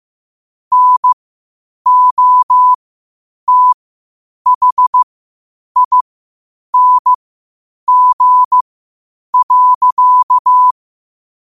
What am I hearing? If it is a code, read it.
Morse code, heard as a keyed tone, about 15 wpm, spaced as for 8 wpm, 1000 Hz. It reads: NOTHING.
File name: qotdfiles / 8wpm